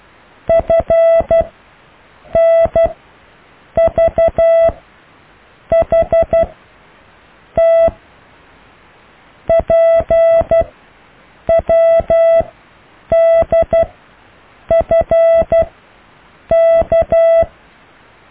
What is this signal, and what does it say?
Morse code, heard as a keyed tone, about 12 wpm, spaced as for 7 wpm, 660 Hz. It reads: FNVHT PWDFK
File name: cw / EKTPV+BFHNW